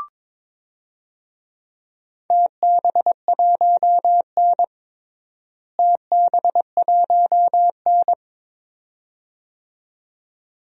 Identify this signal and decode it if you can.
Morse code, heard as a keyed tone, about 22 wpm, 700 Hz. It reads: TB1N TB1N
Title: DXCC-100 Call Signs - 4 Characters - Repeated 2x - 22wpm